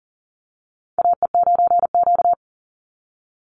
Morse code, heard as a keyed tone, about 40 wpm, 700 Hz. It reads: AE9Q
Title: NA SOTA Chaser Callsigns 40wpm